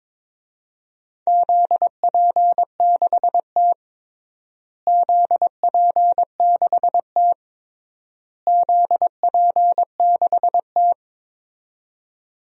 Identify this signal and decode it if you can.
Morse code, heard as a keyed tone, about 22 wpm, 700 Hz. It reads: ZP6T ZP6T ZP6T
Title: DXCC-100 Call Signs - 4 Characters - Repeated 3x - 22wpm